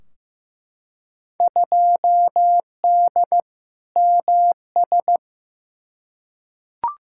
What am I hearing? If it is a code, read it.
Morse code, heard as a keyed tone, about 15 wpm, 700 Hz. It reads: 2D MS
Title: Field Day Exchanges Encore - 15wpm